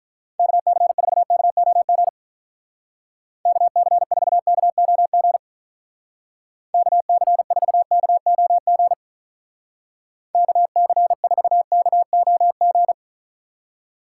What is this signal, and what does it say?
Morse code, heard as a keyed tone, about 44 wpm, 700 Hz. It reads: KC4KOG KC4KOG KC4KOG KC4KOG
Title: Speed Racing - USA 2x3 Call Signs 35wpm